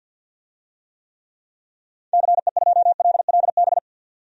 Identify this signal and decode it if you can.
Morse code, heard as a keyed tone, about 53 wpm, 700 Hz. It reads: CE2LLB
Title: DXCC-100 Call Signs - 6 Characters - 50wpm